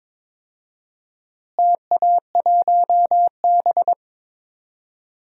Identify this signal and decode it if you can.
Morse code, heard as a keyed tone, about 22 wpm, 700 Hz. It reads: TA1B
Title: DXCC-100 Call Signs - 4 Characters - Repeated 3x - 22wpm